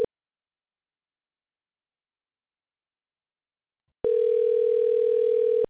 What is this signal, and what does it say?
Ringback tone (North America), 440 + 480 Hz.